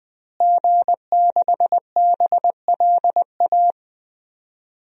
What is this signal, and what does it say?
Morse code, heard as a keyed tone, about 20 wpm, 700 Hz. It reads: G6BLA
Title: DXCC-100 Call Signs - 5 Characters - Repeated 2x - 20wpm